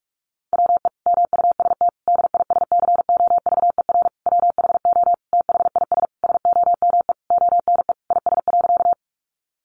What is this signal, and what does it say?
Morse code, heard as a keyed tone, about 44 wpm, 700 Hz. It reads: WE MUST DISCOVER WHO THIS SOME ONE IS.
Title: Sentences from Top 1100 Words 45wpm